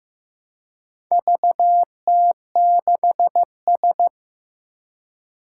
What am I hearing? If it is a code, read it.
Morse code, heard as a keyed tone, about 15 wpm, 700 Hz. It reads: VT6S
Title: Four Challenging Letter-Number Combinations 15wpm